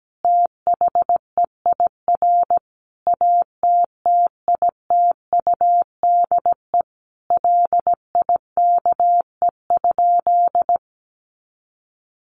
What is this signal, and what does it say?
Morse code, heard as a keyed tone, about 17 wpm, 700 Hz. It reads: THEIR ATTITUDE LIKE?
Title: Sentences from Top 1300 Words 17wpm